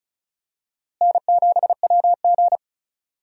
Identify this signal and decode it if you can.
Morse code, heard as a keyed tone, about 34 wpm, 700 Hz. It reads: N7WG
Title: USA Call Signs - 4 Characters - Repeated 2x - 35wpm